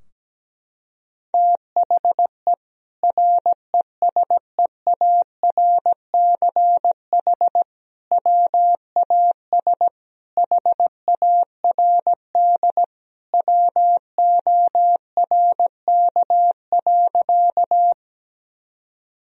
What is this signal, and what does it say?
Morse code, heard as a keyed tone, about 17 wpm, 700 Hz. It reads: THE RESEARCH WAS HARD WORK.
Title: Sentences from Top 1000 Words - Review Part 4 - 17wpm